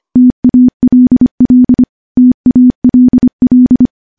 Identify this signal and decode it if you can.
Morse code, heard as a keyed tone, about 25 wpm, 250 Hz. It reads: TALL TALL
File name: morse-training-25WPM